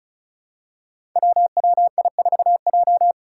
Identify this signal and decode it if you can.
Morse code, heard as a keyed tone, about 36 wpm, 700 Hz. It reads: WWI4J
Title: Five Challenging Letter-Number Combinations 35wpm